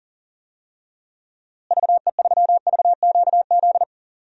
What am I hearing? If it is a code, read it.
Morse code, heard as a keyed tone, about 40 wpm, 700 Hz. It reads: VE3VQZ
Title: DXCC-100 Call Signs - 6 Characters - Repeated 2x - 40wpm